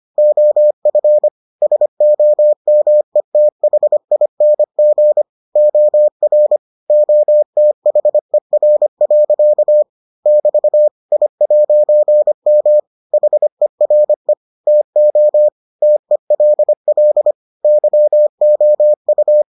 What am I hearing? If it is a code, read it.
Morse code, heard as a keyed tone, about 25 wpm, 600 Hz. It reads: OF SOMETHING OR OTHER. <BT> I'M HERE TO TELL YOU